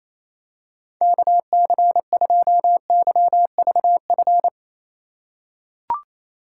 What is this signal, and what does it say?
Morse code, heard as a keyed tone, about 28 wpm, 700 Hz. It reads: KC2YVF
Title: USA Call Signs - 3x Character Spacing - 28wpm